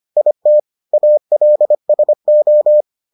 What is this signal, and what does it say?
Morse code, heard as a keyed tone, about 25 wpm, 600 Hz. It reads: IT ALSO